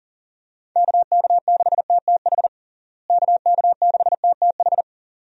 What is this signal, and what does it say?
Morse code, heard as a keyed tone, about 40 wpm, 700 Hz. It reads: KK6TTH KK6TTH
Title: USA Call Signs - 6 Characters - Repeated 2x - 40wpm